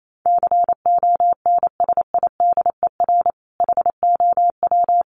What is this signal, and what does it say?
Morse code, heard as a keyed tone, about 28 wpm, 700 Hz. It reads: CONSIDER HOW
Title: Sentences from Top 1000 Words 28wpm